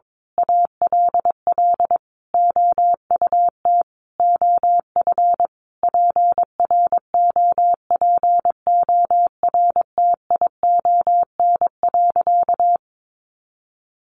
Morse code, heard as a keyed tone, about 22 wpm, 700 Hz. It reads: ALL OUT OF PROPORTION.
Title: A Sound Of Thunder 22wpm